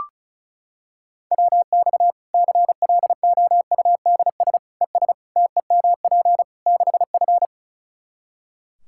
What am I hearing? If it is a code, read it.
Morse code, heard as a keyed tone, about 36 wpm, 700 Hz. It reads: WX CLOUDS ES TEMP 6F
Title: Practice QSO - Exchange 2 - Info - 35wpm